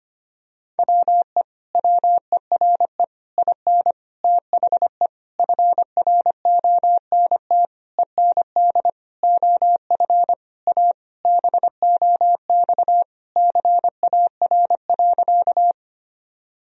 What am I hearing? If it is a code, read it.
Morse code, heard as a keyed tone, about 25 wpm, 700 Hz. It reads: WE WERE IN THE FRONT END OF A BOX CAR.
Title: Sentences from Top 1000 Words - Review Part 2 - 25wpm